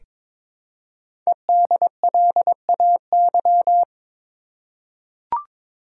Morse code, heard as a keyed tone, about 22 wpm, 700 Hz. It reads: EDLAY
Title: Code Groups - 5 Letters - 22wpm